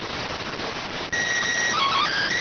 SSTV (slow-scan television), PD 290.
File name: findings / 100 surview 0.gen-13 rx